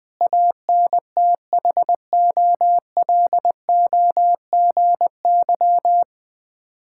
Morse code, heard as a keyed tone, about 20 wpm, 690 Hz. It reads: ANTHOLOGY
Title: Difficult Words with Hints - 20wpm